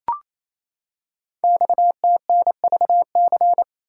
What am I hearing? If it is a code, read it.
Morse code, heard as a keyed tone, about 28 wpm, 700 Hz. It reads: XTNVC
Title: Code Groups - 5 Letter:Numbers - 28wpm